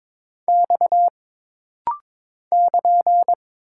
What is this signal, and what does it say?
Morse code, heard as a keyed tone, about 22 wpm, 700 Hz. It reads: X <KN>
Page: This rapid-fire timing is geared to push your ICR (Instant Character Recognition) skills.